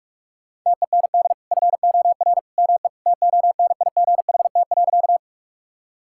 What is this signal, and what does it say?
Morse code, heard as a keyed tone, about 41 wpm, 700 Hz. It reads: TEND FOR ME TONIGHT.
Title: Sentences from Top 1000 Words - Review Part 5 - 45wpm